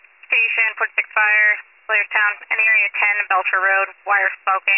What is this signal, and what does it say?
Emergency Alert System (SAME) tones, heard as harsh buzzing bursts, with speech. It is heard from 0.5 to 3.9 s.